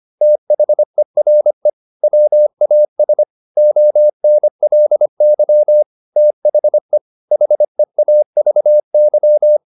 Morse code, heard as a keyed tone, about 25 wpm, 600 Hz. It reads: THERE WAS ONLY THE HEAVY